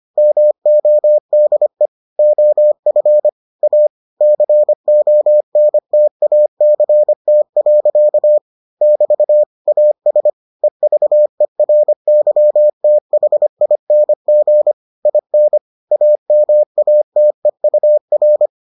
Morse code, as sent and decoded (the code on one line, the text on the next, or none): -- --- -.. / --- ..-. / .- / -.-. --- -. - .- -.-. - .-.-.- / -...- / .- ... / . ...- . .-. -.-- - .... .. -. --. / .. -. / .- -- .- - . ..- .-.
MODE OF A CONTACT. <BT> AS EVERYTHING IN AMATEUR